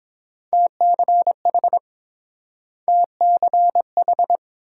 Morse code, heard as a keyed tone, about 29 wpm, 700 Hz. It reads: TCH TCH